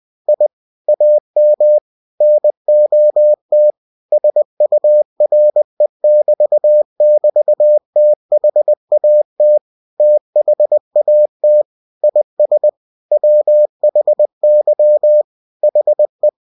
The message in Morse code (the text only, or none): I AM NOT SURE<BT><BT>THAT THAT IS WHY HE